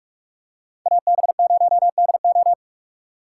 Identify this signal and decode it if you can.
Morse code, heard as a keyed tone, about 46 wpm, 700 Hz. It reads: AB0DO